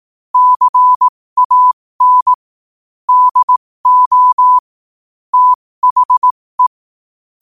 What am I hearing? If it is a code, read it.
Morse code, heard as a keyed tone, about 18 wpm, spaced as for 15 wpm, 1000 Hz. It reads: CAN DO THE
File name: qotdfiles / 15wpm